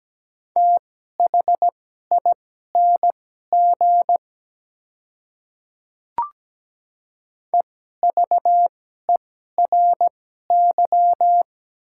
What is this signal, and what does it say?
Morse code, heard as a keyed tone, about 17 wpm, spaced as for 12 wpm, 700 Hz. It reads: THING EVERY
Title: 2x Character Spacing - Top 200 Words - 17wpm